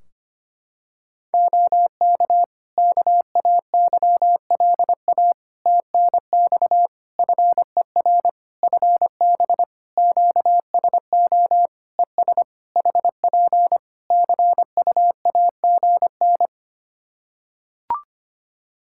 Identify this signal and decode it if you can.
Morse code, heard as a keyed tone, about 25 wpm, 700 Hz. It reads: OK KAYLA TNX FER FB QSO ES HP CUAGN
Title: Practice QSO - Exchange 4 - Ending - 25wpm